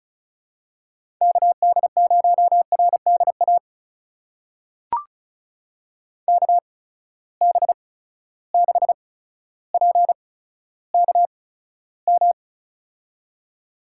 Morse code, heard as a keyed tone, about 36 wpm, spaced as for 15 wpm, 700 Hz. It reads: KD0RDA KB6PKM
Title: USA Call Signs - 8x Character Spacing - 35wpm